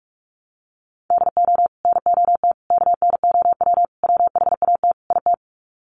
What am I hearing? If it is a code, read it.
Morse code, heard as a keyed tone, about 44 wpm, 700 Hz. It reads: DO NOT KNOW WHAT IT
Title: Sets of 5 Words 45wpm